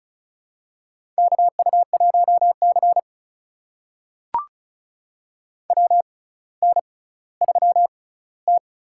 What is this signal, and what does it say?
Morse code, heard as a keyed tone, about 34 wpm, spaced as for 18 wpm, 700 Hz. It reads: KU1C WN3T
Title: USA Call Signs - 4 Characters - 6x Character Spacing - 35wpm